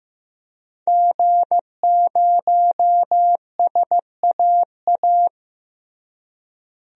Morse code, heard as a keyed tone, about 15 wpm, 700 Hz. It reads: G0SAA
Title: DXCC-100 Call Signs - 5 Characters - Repeated 2x - 15wpm